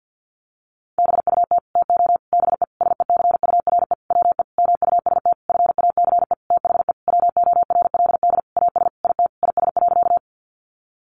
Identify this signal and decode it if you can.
Morse code, heard as a keyed tone, about 49 wpm, 700 Hz. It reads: BUT TO BE SECURE WE MUST FACE THE WORLD AS IT IS.
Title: Sentences from Top 1200 Words 50wpm